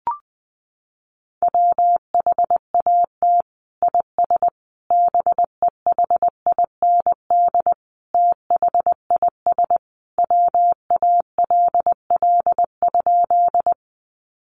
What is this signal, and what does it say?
Morse code, heard as a keyed tone, about 20 wpm, 700 Hz. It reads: WHAT IS BEHIND THIS WALL?
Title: Sentences from Top 1100 Words 20wpm